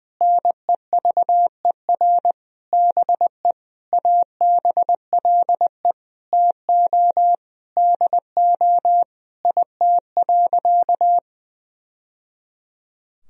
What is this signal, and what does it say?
Morse code, heard as a keyed tone, about 20 wpm, 700 Hz. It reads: NEVER BE ABLE TO DO IT.
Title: Sentences from Top 1000 Words - Review Part 2 - 20wpm